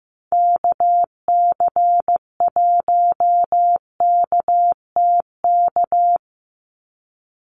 Morse code, heard as a keyed tone, about 15 wpm, 700 Hz. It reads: KC1KTK
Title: USA Call Signs 15wpm